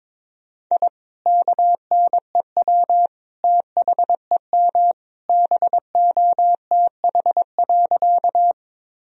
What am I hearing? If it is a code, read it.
Morse code, heard as a keyed tone, about 22 wpm, 700 Hz. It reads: I KNEW THEM BOTH.
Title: Sentences from Top 1000 Words - Review Part 1 - 22wpm